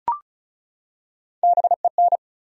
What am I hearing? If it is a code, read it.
Morse code, heard as a keyed tone, about 36 wpm, 700 Hz. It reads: BEN
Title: USA Common Names - Max 4 Letters - 35wpm